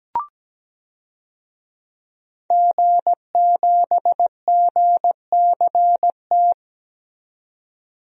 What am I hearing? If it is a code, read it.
Morse code, heard as a keyed tone, about 17 wpm, 700 Hz. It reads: G7GCT
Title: DXCC-100 Call Signs - 5 Characters - 17wpm